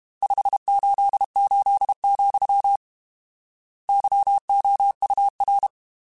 Morse code, heard as a keyed tone, about 32 wpm, 800 Hz. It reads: 588, YOUR